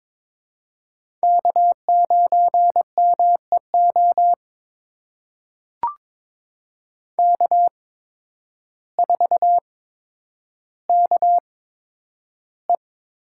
Morse code, heard as a keyed tone, about 22 wpm, spaced as for 10 wpm, 700 Hz. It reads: K9MEO K4KE
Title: USA Call Signs - 5 Characters - 8x Character Spacing - 22wpm